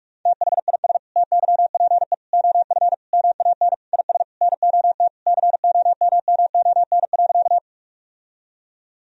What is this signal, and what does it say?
Morse code, heard as a keyed tone, about 42 wpm, 700 Hz. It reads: THIS TYPE OF MAN IS NOT COMMON.
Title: Sentences from Top 1000 Words - Review Part 3 - 45wpm